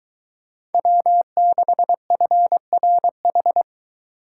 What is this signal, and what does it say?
Morse code, heard as a keyed tone, about 23 wpm, 700 Hz. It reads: W6FRH